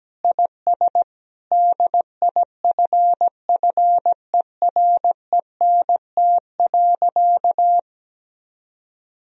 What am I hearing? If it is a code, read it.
Morse code, heard as a keyed tone, about 17 wpm, 700 Hz. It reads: IS DIFFERENT.
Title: Sentences from Top 1000 Words - Review Part 3 - 17wpm